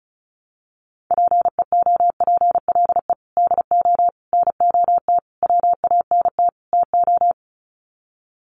Morse code, heard as a keyed tone, about 34 wpm, 700 Hz. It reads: PEOPLE DO NOT WANT TO
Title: Sets of 5 Words Encore2 35wpm